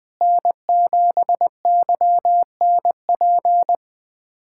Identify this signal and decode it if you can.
Morse code, heard as a keyed tone, about 20 wpm, 700 Hz. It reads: N7YNP
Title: USA Callsign DE Callsign - 5 Characters - 20wpm